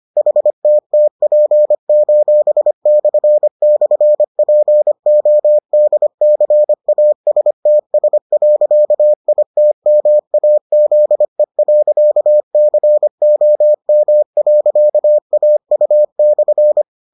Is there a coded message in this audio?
Yes: Morse code from 0.2 to 16.8 s, heard as a keyed tone, about 25 wpm, 600 Hz. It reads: HTTP://PODCASTS.ITMAZE.COM.AU/